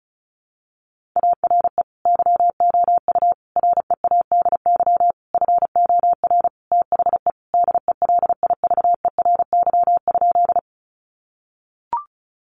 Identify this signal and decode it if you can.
Morse code, heard as a keyed tone, about 34 wpm, 700 Hz. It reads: ARE YOU READY FOR THE DELIVERY?
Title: Sentences from Top 1500 Words 35wpm